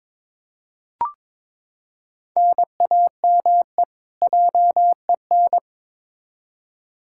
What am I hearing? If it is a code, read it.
Morse code, heard as a keyed tone, about 22 wpm, 700 Hz. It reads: NAME JEN
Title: K1USN SST QSO Practice 22wpm